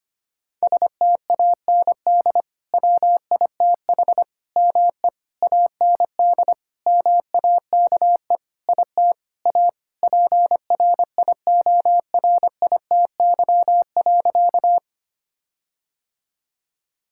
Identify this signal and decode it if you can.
Morse code, heard as a keyed tone, about 25 wpm, 700 Hz. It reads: STAND WITH ME AND MAKE IT A PRIORITY.
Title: Sentences from Top 1600 Words 25wpm